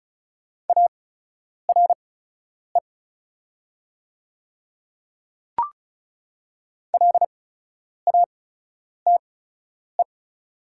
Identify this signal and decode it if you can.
Morse code, heard as a keyed tone, about 36 wpm, spaced as for 10 wpm, 700 Hz. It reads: ARE LATE
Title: 8x Character Spacing - Top 300 Words - 35wpm